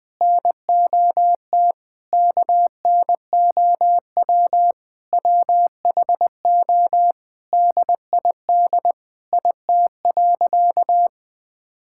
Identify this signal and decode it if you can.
Morse code, heard as a keyed tone, about 20 wpm, 700 Hz. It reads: NOT KNOW WHO DID IT.
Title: Sentences with 5-Word Phrases - Spoken Beginning - 20wpm